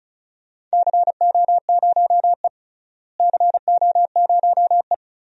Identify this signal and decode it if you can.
Morse code, heard as a keyed tone, about 36 wpm, 700 Hz. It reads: CO0E CO0E
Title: DXCC-100 Call Signs - 4 Characters - Repeated 3x - 35wpm